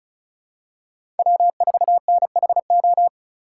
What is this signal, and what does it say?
Morse code, heard as a keyed tone, about 36 wpm, 700 Hz. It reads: W4NHO
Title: USA Call Signs - 5 Characters - 3x Character Spacing - 35wpm